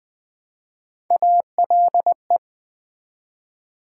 Morse code, heard as a keyed tone, about 20 wpm, 700 Hz. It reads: ALE
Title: Build A Word - 1 Letter At A Time - 5-Letter Words - 20wpm